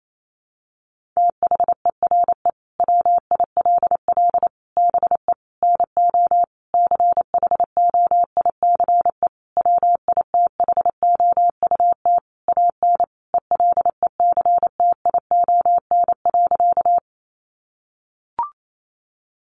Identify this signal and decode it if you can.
Morse code, heard as a keyed tone, about 28 wpm, 700 Hz. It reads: THERE WILL BE NO CHOICE WITHOUT AN ELECTION.
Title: Sentences from Top 1500 Words 28wpm